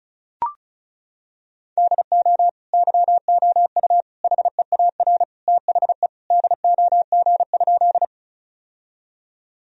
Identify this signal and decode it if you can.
Morse code, heard as a keyed tone, about 36 wpm, 700 Hz. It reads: DO YOU HEAR THE DOG?
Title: Sentences from Top 1000 Words - Review Part 2 - 35wpm